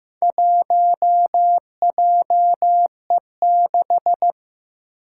Morse code, heard as a keyed tone, about 15 wpm, 700 Hz. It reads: 1JE6
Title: Four Challenging Letter-Number Combinations 15wpm